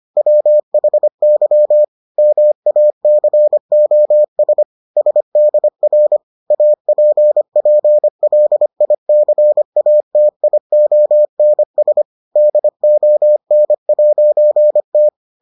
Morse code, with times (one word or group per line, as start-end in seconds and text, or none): 0.2-1.8 WHY
2.2-4.6 MACOS
5.0-6.2 SDR
6.5-12.0 APPLICATIONS
12.4-15.1 DON'T